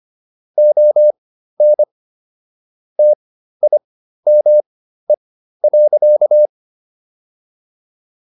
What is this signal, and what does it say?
Morse code, heard as a keyed tone, about 25 wpm, spaced as for 13 wpm, 600 Hz. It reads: ON TIME.